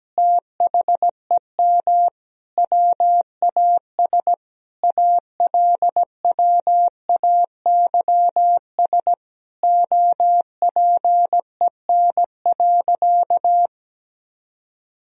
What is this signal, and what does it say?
Morse code, heard as a keyed tone, about 17 wpm, 700 Hz. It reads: THEM WAS ALWAYS OPEN.
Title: Sentences from Top 1000 Words - Review Part 1 - 17wpm